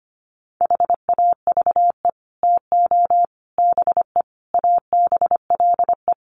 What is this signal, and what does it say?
Morse code, heard as a keyed tone, about 25 wpm, 700 Hz. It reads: HAVE TO BE ABLE
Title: Sets of 5 Words 25wpm